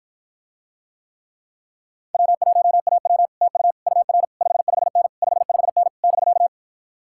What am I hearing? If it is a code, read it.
Morse code, heard as a keyed tone, about 60 wpm, 700 Hz. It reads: W1AW TU UR 55N 55N <BK>